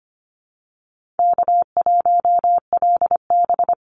Morse code, heard as a keyed tone, about 25 wpm, 700 Hz. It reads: K1LB
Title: NA SOTA Chaser Callsigns 25wpm